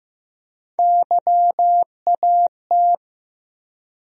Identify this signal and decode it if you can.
Morse code, heard as a keyed tone, about 15 wpm, 700 Hz. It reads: YAT
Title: USA Call Sign Postfixes - 3 Letters 15wpm